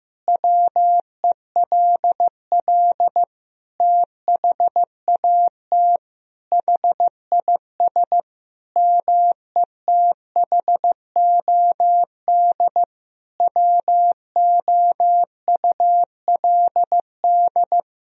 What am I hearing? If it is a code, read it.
Morse code, heard as a keyed tone, about 16 wpm, 700 Hz. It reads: WELL THAT HIS METHOD WOULD
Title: Sentences from Top 1000 Words - Review Part 3 - 15wpm